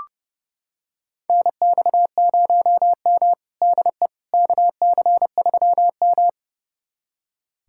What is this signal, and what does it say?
Morse code, heard as a keyed tone, about 30 wpm, 700 Hz. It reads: NX0M DE KC3M
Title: USA Callsign DE Callsign - 4 Characters - 30wpm